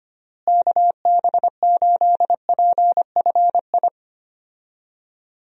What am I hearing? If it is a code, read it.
Morse code, heard as a keyed tone, about 25 wpm, 700 Hz. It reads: KB8PFI
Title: USA Call Signs - 6 Characters - Repeated 2x - 25wpm